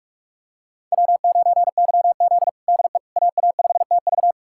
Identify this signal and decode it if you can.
Morse code, heard as a keyed tone, about 46 wpm, 700 Hz. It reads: W9YZ DE AA5TV
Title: USA Callsign DE Callsign 45wpm